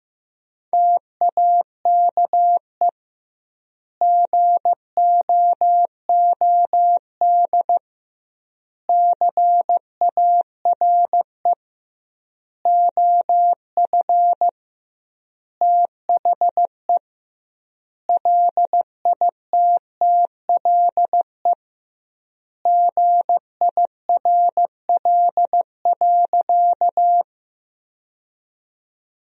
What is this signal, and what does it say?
Morse code, heard as a keyed tone, about 16 wpm, 700 Hz. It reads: TAKE GOOD CARE OF THE LITTLE GIRL.
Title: 2x Word-Spacing - Sentences from Top 400 Words - 15wpm